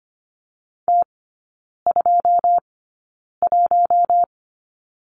Morse code, heard as a keyed tone, about 25 wpm, spaced as for 9 wpm, 690 Hz. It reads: T21
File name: Lesson 15e - Number 5 - Three Characters 25wpm